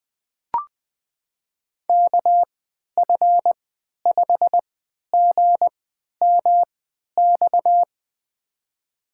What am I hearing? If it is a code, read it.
Morse code, heard as a keyed tone, about 20 wpm, spaced as for 11 wpm, 700 Hz. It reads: KF5GMX
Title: USA Call Signs - 6 Characters - 3x Character Spacing - 20wpm